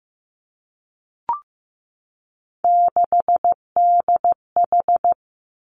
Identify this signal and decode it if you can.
Morse code, heard as a keyed tone, about 15 wpm, 700 Hz. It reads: 6DH
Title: Three Challenging Letter-Number Combinations 15wpm